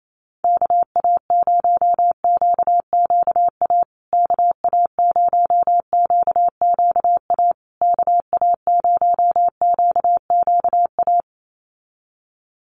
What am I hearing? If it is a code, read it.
Morse code, heard as a keyed tone, about 28 wpm, 700 Hz. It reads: KA0QQA KA0QQA KA0QQA
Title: USA Call Signs Repeated 3x 28wpm